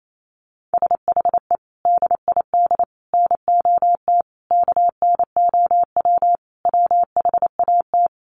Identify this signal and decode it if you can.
Morse code, heard as a keyed tone, about 28 wpm, 700 Hz. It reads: SHE DID NOT KNOW WHAT
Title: Sets of 5 Words Encore2 28wpm